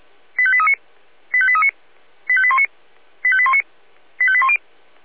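Five-tone selective call (ZVEI), heard as quick runs of five tones, tones of about 70 ms. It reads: 85839 85839 85819 85819 85810
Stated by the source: ZVEI five-tone call 85855 received by SDR.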